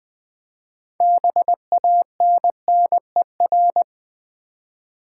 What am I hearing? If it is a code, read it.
Morse code, heard as a keyed tone, about 20 wpm, 700 Hz. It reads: BANNER
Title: 6 Letter Words - First 3 Letters Given - 20wpm